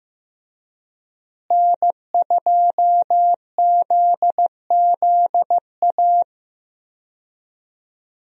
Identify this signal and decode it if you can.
Morse code, heard as a keyed tone, about 15 wpm, 700 Hz. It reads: N2ZZA
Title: USA Call Signs - 5 Characters - Repeated 2x - 15wpm